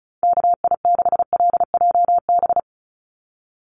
Morse code, heard as a keyed tone, about 34 wpm, 700 Hz. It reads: KI6LJB
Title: USA Call Signs 35wpm